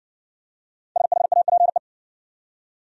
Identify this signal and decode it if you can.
Morse code, heard as a keyed tone, about 58 wpm, 700 Hz. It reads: SHAPE